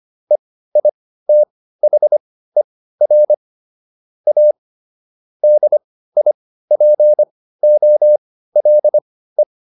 Morse code, heard as a keyed tone, about 25 wpm, spaced as for 15 wpm, 600 Hz. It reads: EITHER A DIPOLE